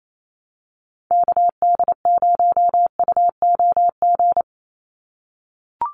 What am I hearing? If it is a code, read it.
Morse code, heard as a keyed tone, about 28 wpm, 700 Hz. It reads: KD0UOG
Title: USA Call Signs 28wpm